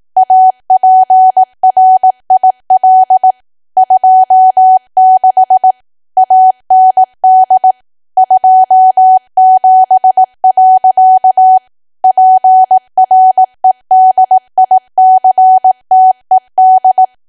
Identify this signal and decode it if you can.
Morse code, heard as a keyed tone, about 18 wpm, 750 Hz. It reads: APRIL 26 AND 27. PREDICTED
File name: cwa / W1AW-Level2 / 18WPM